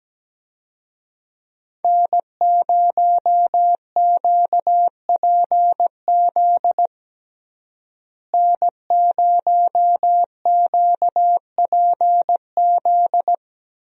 Morse code, heard as a keyed tone, about 17 wpm, 700 Hz. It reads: N0QPZ N0QPZ
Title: USA Call Signs - 5 Characters - Repeated 3x - 17wpm